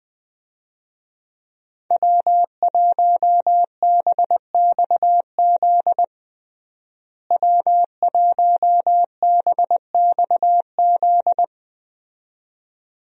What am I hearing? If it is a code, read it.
Morse code, heard as a keyed tone, about 20 wpm, 700 Hz. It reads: W1BXZ W1BXZ
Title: USA Call Signs - 5 Characters - Repeated 2x - 20wpm